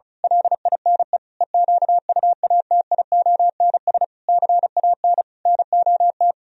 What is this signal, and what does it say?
Morse code, heard as a keyed tone, about 32 wpm, 700 Hz. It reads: LINE EQUATIONS CAN NOT
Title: Speed-Racing - Sets of 5 Words 35wpm